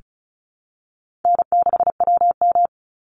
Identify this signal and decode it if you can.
Morse code, heard as a keyed tone, about 34 wpm, 700 Hz. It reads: N6WM
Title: Call Signs Repeated 3x - 35wpm